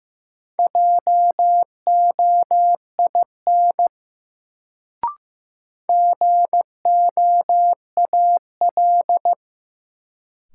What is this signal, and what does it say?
Morse code, heard as a keyed tone, about 15 wpm, 700 Hz. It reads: JOIN GOAL
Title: Sentences from Top 1400 Words - New Word Drill Alt - 15wpm